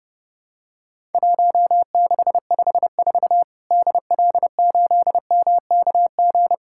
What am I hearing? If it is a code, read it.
Morse code, heard as a keyed tone, about 30 wpm, 700 Hz. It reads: 1654 DL8MKG
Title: Worked All Europe DX Contest - QTC Exchange - 30wpm